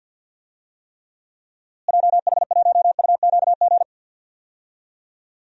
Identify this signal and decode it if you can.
Morse code, heard as a keyed tone, about 47 wpm, 700 Hz. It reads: JH1UQG
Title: DXCC-100 Call Signs - 6 Characters - Repeated 3x - 50wpm